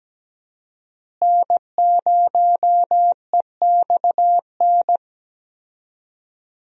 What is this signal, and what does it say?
Morse code, heard as a keyed tone, about 17 wpm, 700 Hz. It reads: N0EXN
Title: USA Call Signs - 5 Characters - Repeated 2x - 17wpm